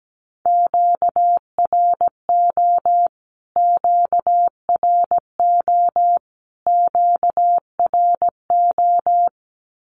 Morse code, heard as a keyed tone, about 17 wpm, 700 Hz. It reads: QRO QRO QRO
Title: Q-Codes Repeated 3x 17wpm